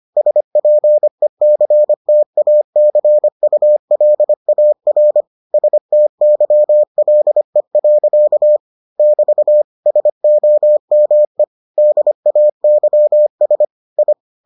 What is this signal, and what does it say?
Morse code, heard as a keyed tone, about 25 wpm, 600 Hz. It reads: SPECTACULAR STYLE. <BT> SOME DAYS I